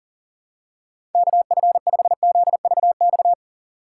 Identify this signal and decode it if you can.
Morse code, heard as a keyed tone, about 40 wpm, 700 Hz. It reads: KF5ZVX
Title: USA Call Signs - 6 Characters - 6x Character Spacing - 40wpm